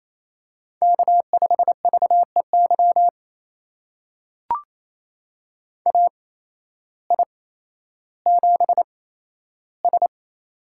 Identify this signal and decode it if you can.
Morse code, heard as a keyed tone, about 28 wpm, spaced as for 13 wpm, 700 Hz. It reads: K5VEY AI7S
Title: USA Call Signs - 8x Character Spacing - 28wpm